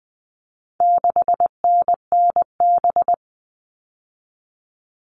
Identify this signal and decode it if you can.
Morse code, heard as a keyed tone, about 20 wpm, 700 Hz. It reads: 6NNB
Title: All Cut Numbers - 4-Digit 20wpm